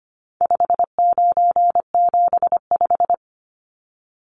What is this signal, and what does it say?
Morse code, heard as a keyed tone, about 25 wpm, 700 Hz. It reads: 5975
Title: Numbers - 4-Digit 25wpm